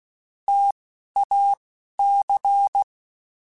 Morse code, heard as a keyed tone, about 16 wpm, spaced as for 11 wpm, 800 Hz. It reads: TAC